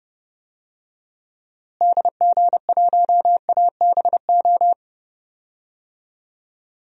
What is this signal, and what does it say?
Morse code, heard as a keyed tone, about 30 wpm, 700 Hz. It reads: DG1ABO
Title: DXCC-100 Call Signs - 6 Characters - Repeated 3x - 30wpm